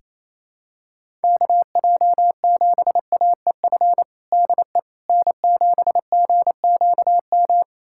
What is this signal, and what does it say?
Morse code, heard as a keyed tone, about 27 wpm, 700 Hz. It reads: KJ7AEF DE N7GQM
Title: USA Callsign DE Callsign 28wpm